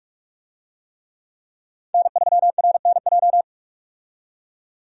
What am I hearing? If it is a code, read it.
Morse code, heard as a keyed tone, about 46 wpm, 700 Hz. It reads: N3RNJ
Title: USA Call Signs - 5 Characters - 45wpm